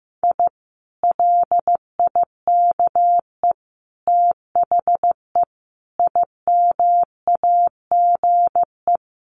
Morse code, heard as a keyed tone, about 15 wpm, 700 Hz. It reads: I LIKE THE IMAGE
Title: Sentences from Top 700 Words 15wpm